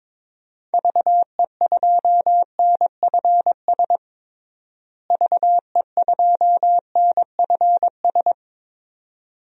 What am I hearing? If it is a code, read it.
Morse code, heard as a keyed tone, about 22 wpm, 700 Hz. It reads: VE2NFS VE2NFS
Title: DXCC-100 Call Signs - 6 Characters - Repeated 2x - 22wpm